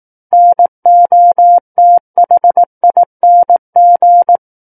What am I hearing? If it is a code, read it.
Morse code, heard as a keyed tone, about 18 wpm, 700 Hz. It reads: NOTHING
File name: 18 WPM